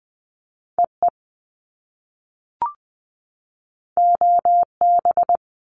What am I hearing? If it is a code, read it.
Morse code, heard as a keyed tone, about 20 wpm, 700 Hz. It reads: EE OB
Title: Top Two Letter Combinations in Words 20wpm